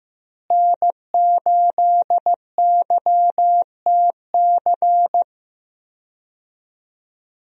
Morse code, heard as a keyed tone, about 15 wpm, 700 Hz. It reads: N8YTC
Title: USA Call Signs - 5 Characters - Repeated 2x - 15wpm